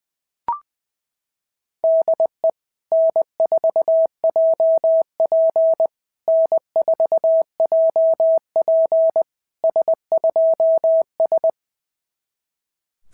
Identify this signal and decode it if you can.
Morse code, heard as a keyed tone, about 21 wpm, 650 Hz. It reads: DE N4JP N4JP S2S
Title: SOTA QSO Practice 20wpm